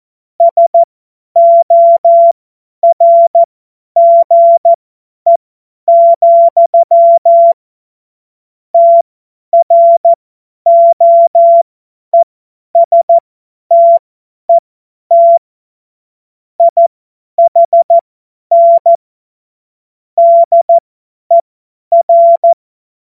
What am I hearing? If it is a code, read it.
Morse code, heard as a keyed tone, about 14 wpm, spaced as for 10 wpm, 680 Hz. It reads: SORGE, TROESTET IHN DER
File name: Morsekurs / Witze / Witz 05 März 2024 50 F70